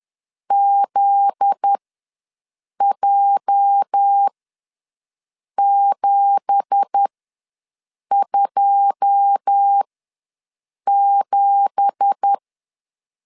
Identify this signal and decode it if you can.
Morse code, heard as a keyed tone, about 10 wpm, spaced as for 6 wpm, 800 Hz. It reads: ZJ727
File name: LEZIONI / 9 2867 ws 10wpm 4215